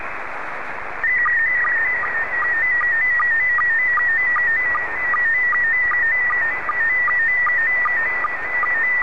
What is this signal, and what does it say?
SSTV (slow-scan television), PD 50.
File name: sstv / PD50